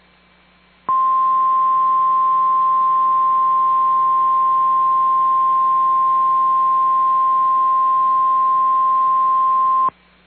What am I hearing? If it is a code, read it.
Emergency Alert System (SAME) tones (the 1050 Hz attention tone). It is heard from 0.8 to 9.9 s.
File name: EAS Logs / KMNR